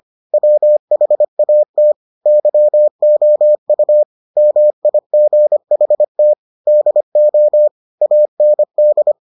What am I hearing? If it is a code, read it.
Morse code, heard as a keyed tone, about 25 wpm, 600 Hz. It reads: WHAT YOU MIGHT DO AND